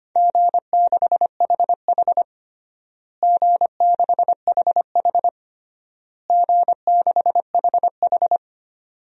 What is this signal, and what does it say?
Morse code, heard as a keyed tone, about 25 wpm, 700 Hz. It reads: G6HH G6HH G6HH
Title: DXCC-100 Call Signs - 4 Characters - Repeated 3x - 25wpm